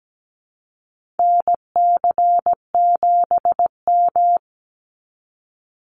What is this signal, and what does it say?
Morse code, heard as a keyed tone, about 17 wpm, 700 Hz. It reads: NC7M
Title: Call Signs Repeated 3x - 17wpm